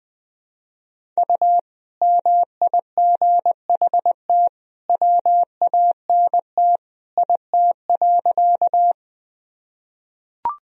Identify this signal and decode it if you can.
Morse code, heard as a keyed tone, about 21 wpm, 700 Hz. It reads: U MIGHT WANT IT.
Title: Sentences from Top 300 Words and Abbreviations 20wpm